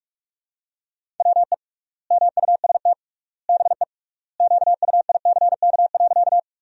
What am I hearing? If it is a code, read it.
Morse code, heard as a keyed tone, about 46 wpm, 700 Hz. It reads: WE MUST BE QUICK.
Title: 3x Word-Spacing - Sentences from Top 500 Words - 45wpm